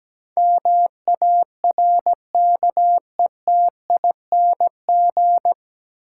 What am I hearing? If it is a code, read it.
Morse code, heard as a keyed tone, about 17 wpm, 700 Hz. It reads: MARKETING
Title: Sentences from Top 1000 Words - Review Part 5 - New Word Drill - 17wpm